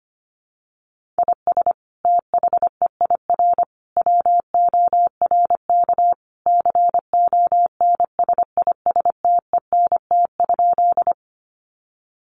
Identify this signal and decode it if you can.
Morse code, heard as a keyed tone, about 25 wpm, 700 Hz. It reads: IS THEIR WORK CONSISTENT?
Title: Sentences from Top 1500 Words 25wpm